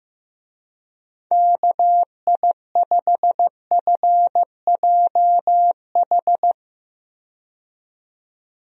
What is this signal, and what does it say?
Morse code, heard as a keyed tone, about 15 wpm, 700 Hz. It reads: KI5FJH
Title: USA Call Signs - 6 Characters - 15wpm